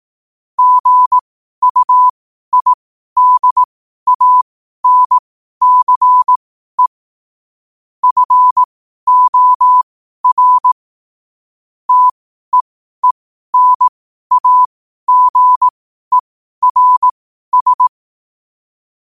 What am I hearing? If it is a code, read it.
Morse code, heard as a keyed tone, about 18 wpm, spaced as for 12 wpm, 1000 Hz. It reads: GUIDANCE FOR TEENAGERS
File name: qotdfiles / 12wpm